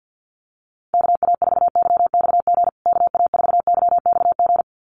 Morse code, heard as a keyed tone, about 50 wpm, 700 Hz. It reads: KA4YXG KA4YXG
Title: USA Call Signs Repeated 2x 50wpm